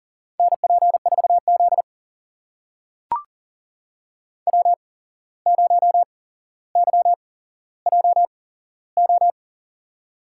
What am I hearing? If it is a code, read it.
Morse code, heard as a keyed tone, about 37 wpm, spaced as for 16 wpm, 700 Hz. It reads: NP4Z W0YJO
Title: USA Call Signs - 8x Character Spacing - 40wpm